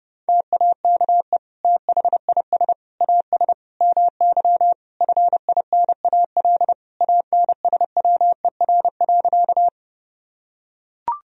Morse code, heard as a keyed tone, about 30 wpm, 700 Hz. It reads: TAKE THIS AS MY FINAL ANSWER.
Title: Sentences from Top 1000 Words - Review Part 2 - 30wpm